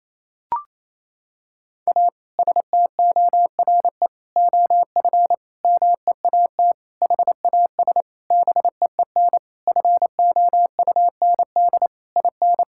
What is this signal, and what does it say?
Morse code, heard as a keyed tone, about 27 wpm, 700 Hz. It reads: A STORE OF MEAT HAS BEEN FOUND IN
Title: Sentences from Top 1000 Words - Review Part 5 - 28wpm